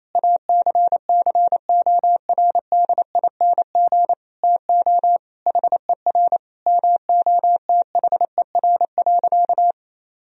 Morse code, heard as a keyed tone, about 28 wpm, 700 Hz. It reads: ACCORDING TO HER MOTHER.
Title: Sentences from Top 1000 Words - Review Part 4 - 28wpm